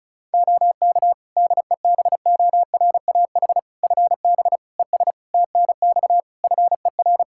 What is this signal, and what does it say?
Morse code, heard as a keyed tone, about 36 wpm, 700 Hz. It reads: OK DEBORAH FB ES TNX FER
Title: Practice QSO - Exchange 2 - Info - 35wpm